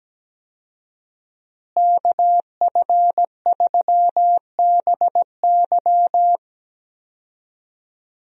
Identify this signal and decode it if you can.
Morse code, heard as a keyed tone, about 17 wpm, 700 Hz. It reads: KF3BY
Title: USA Call Signs - 5 Characters - Repeated 3x - 17wpm